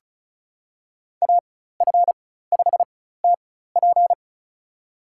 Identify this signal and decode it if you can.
Morse code, heard as a keyed tone, about 34 wpm, spaced as for 16 wpm, 700 Hz. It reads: AF5TP
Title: USA Call Signs - 5 Characters - 4x Character Spacing - 35wpm